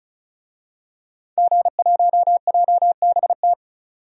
Morse code, heard as a keyed tone, about 34 wpm, 700 Hz. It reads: G1JBT
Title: DXCC-100 Call Signs - 5 Characters - 35wpm